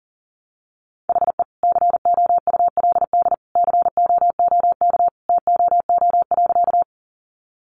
Morse code, heard as a keyed tone, about 40 wpm, 700 Hz. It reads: HE COULD COOK TOO.
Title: Sentences from Top 700 Words 40wpm